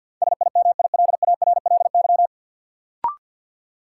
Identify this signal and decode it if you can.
Morse code, heard as a keyed tone, about 53 wpm, 700 Hz. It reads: SIMILARLY
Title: Sentences from Top 1400 Words - New Word Drill Alt - 50wpm